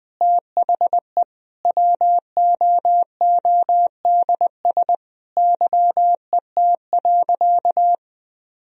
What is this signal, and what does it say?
Morse code, heard as a keyed tone, about 21 wpm, 700 Hz. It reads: THE WOODS YET.
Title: Sentences from Top 1000 Words - Review Part 2 - 20wpm